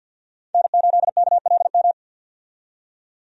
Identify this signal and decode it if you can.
Morse code, heard as a keyed tone, about 50 wpm, 700 Hz. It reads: N8KLM
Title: USA Call Signs - 5 Characters - 50wpm